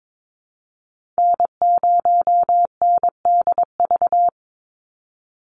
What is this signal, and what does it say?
Morse code, heard as a keyed tone, about 22 wpm, 700 Hz. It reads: N0NDV
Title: USA Call Signs 22wpm